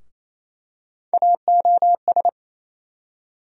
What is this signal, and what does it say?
Morse code, heard as a keyed tone, about 27 wpm, 700 Hz. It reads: AOS
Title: USA Call Sign Postfixes - 3 Letters 28wpm